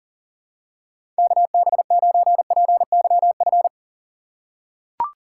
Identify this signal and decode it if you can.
Morse code, heard as a keyed tone, about 40 wpm, 700 Hz. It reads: KB9PYF
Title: USA Call Signs - 6 Characters - 6x Character Spacing - 40wpm